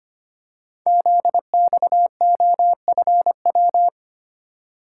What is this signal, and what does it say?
Morse code, heard as a keyed tone, about 25 wpm, 700 Hz. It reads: ZXOFW
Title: Code Groups - 5 Letter:Numbers - 25wpm